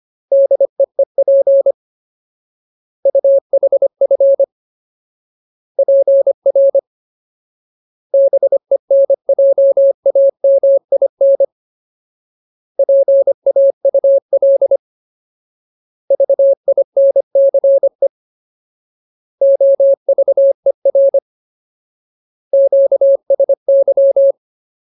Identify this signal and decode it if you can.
Morse code, heard as a keyed tone, about 25 wpm, 550 Hz. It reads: DEEP UHF PR BENJAMIN PAUL VINCE OVER QSY
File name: Ultimate Word List / 25WPM Ultimate